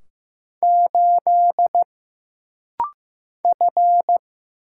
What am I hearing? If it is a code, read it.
Morse code, heard as a keyed tone, about 15 wpm, 700 Hz. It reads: 8 F
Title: Single Letter-Number Alt - Rapid-Fire 15wpm